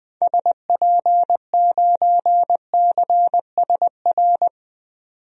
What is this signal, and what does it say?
Morse code, heard as a keyed tone, about 20 wpm, 700 Hz. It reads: SP9CSR
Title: DXCC-100 Call Signs - 6 Characters - 20wpm